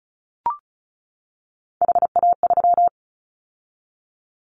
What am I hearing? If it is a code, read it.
Morse code, heard as a keyed tone, about 36 wpm, 690 Hz. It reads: HA3
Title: Callsign Prefixes for Top 50 DXCC - 35wpm